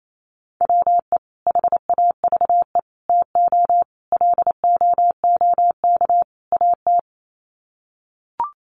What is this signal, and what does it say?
Morse code, heard as a keyed tone, about 28 wpm, 700 Hz. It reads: WE HAVE TO LOOK AT
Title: Sets of 5 Words Encore2 28wpm